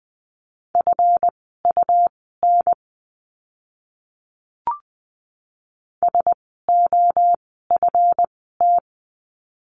Morse code, heard as a keyed tone, about 20 wpm, spaced as for 14 wpm, 700 Hz. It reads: FUN SOFT
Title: 139 Lesson 23f - Letter f - Words 20wpm